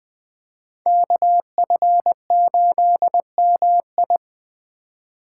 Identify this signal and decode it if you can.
Morse code, heard as a keyed tone, about 21 wpm, 700 Hz. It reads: KF8MI
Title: Speed Racing - USA Call Signs 17wpm